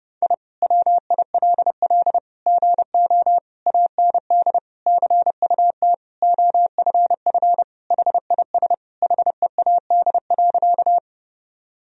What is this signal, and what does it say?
Morse code, heard as a keyed tone, about 30 wpm, 700 Hz. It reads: I WILL GO AND CUT OFF HIS HEAD.
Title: Sentences from Top 1000 Words - Review Part 1 - 30wpm